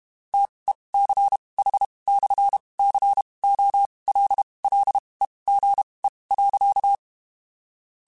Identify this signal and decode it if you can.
Morse code, heard as a keyed tone, about 32 wpm, spaced as for 23 wpm, 800 Hz. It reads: TECH/COLLEGE.
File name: (spaced) / QSOs Spaced-mp3s / WD0BGP 1 @30 sp